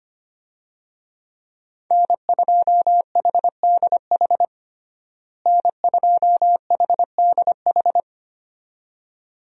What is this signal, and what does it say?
Morse code, heard as a keyed tone, about 25 wpm, 700 Hz. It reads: N2HDH N2HDH
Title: USA Call Signs - 5 Characters - Repeated 2x - 25wpm